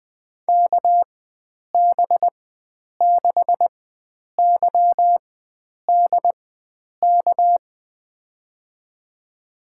Morse code, heard as a keyed tone, about 20 wpm, spaced as for 9 wpm, 700 Hz. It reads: KB6YDK
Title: USA Call Signs - 6 Characters - 4x Character Spacing - 20wpm